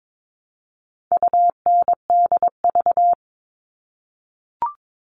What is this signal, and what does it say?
Morse code, heard as a keyed tone, about 22 wpm, 700 Hz. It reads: UNDV
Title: All Cut Numbers - 4-Digit 22wpm